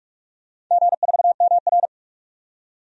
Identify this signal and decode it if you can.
Morse code, heard as a keyed tone, about 44 wpm, 700 Hz. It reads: G4MR
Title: DXCC-100 Call Signs - 4 Characters - 45wpm